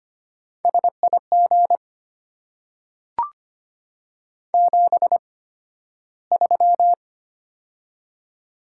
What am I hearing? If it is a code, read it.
Morse code, heard as a keyed tone, about 25 wpm, spaced as for 13 wpm, 700 Hz. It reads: SIG 73
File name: Top 90 QSO Elements - 8x Character Spacing - 25wpm